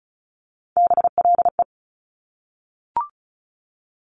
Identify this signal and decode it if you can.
Morse code, heard as a keyed tone, about 36 wpm, 700 Hz. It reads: BLE
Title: Top Three Letter Combinations in Words 35wpm